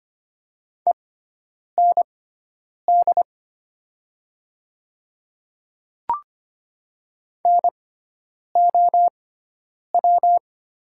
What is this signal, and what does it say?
Morse code, heard as a keyed tone, about 25 wpm, spaced as for 9 wpm, 700 Hz. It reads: END NOW
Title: 6x Character Spacing - Top 300 Words - 25wpm